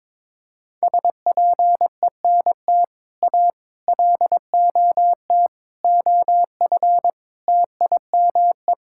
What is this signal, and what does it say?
Morse code, heard as a keyed tone, about 22 wpm, 700 Hz. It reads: SPENT A LOT OF TIME
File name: practice / 22wpm / Build A Phrase - 5-Word Phrases - 22wpm